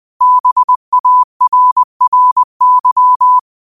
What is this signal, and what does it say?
Morse code, heard as a keyed tone, about 20 wpm, 1000 Hz. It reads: BARRY